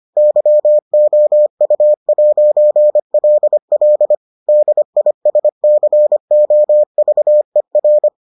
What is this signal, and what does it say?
Morse code, heard as a keyed tone, about 25 wpm, 600 Hz. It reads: YOU'LL DISCOVER